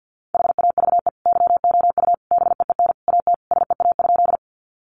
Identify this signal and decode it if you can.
Morse code, heard as a keyed tone, about 50 wpm, 700 Hz. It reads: HAVE YOU BEEN AT SEA?
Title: Sentences from Top 300 Words 50wpm